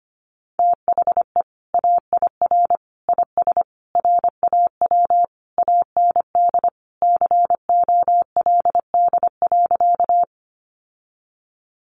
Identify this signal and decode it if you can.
Morse code, heard as a keyed tone, about 25 wpm, 700 Hz. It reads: THE AIR IS RAW AND COLD.
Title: Sentences from Top 1500 Words 25wpm